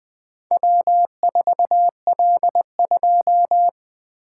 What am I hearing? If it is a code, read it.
Morse code, heard as a keyed tone, about 21 wpm, 700 Hz. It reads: W4L2
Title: Four Letter-Number Combinations - 20wpm